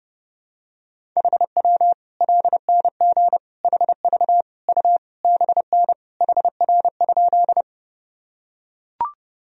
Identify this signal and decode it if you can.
Morse code, heard as a keyed tone, about 30 wpm, 700 Hz. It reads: HW LNG HV U BN HR?
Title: Sentences from Top 200 Words and Abbreviations 30wpm